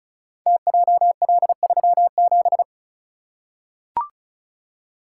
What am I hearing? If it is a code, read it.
Morse code, heard as a keyed tone, about 36 wpm, 690 Hz. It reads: TJL37
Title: Code Groups - 5 Letter:Numbers - 35wpm